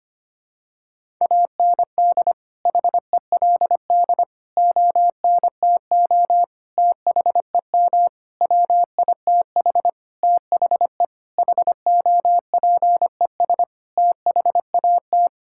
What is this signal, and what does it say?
Morse code, heard as a keyed tone, about 25 wpm, 700 Hz. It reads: AND HELD ONTO THEM WITH THE HOPES THAT
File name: practice / 25wpm / Sentences with 9-Word Phrases - 25wpm